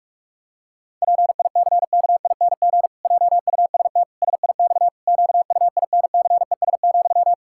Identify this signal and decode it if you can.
Morse code, heard as a keyed tone, about 41 wpm, 700 Hz. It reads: PICKING JUST SIX QUINCES,
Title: Pangram Challenge v2 - 45wpm